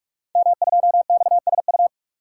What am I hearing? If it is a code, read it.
Morse code, heard as a keyed tone, about 44 wpm, 700 Hz. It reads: M2XSU
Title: Code Groups - 5 Letter:Numbers - 45wpm